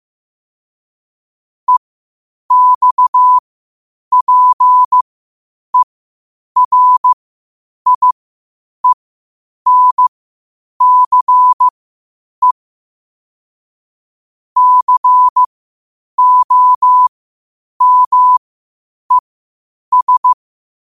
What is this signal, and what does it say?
Morse code, heard as a keyed tone, about 15 wpm, spaced as for 8 wpm, 1000 Hz. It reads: EXPERIENCE COMES
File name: qotdfiles / 8wpm